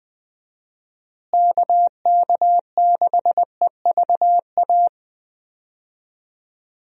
Morse code, heard as a keyed tone, about 20 wpm, 700 Hz. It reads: KK6EVA
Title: USA Call Signs - 6 Characters - Repeated 2x - 20wpm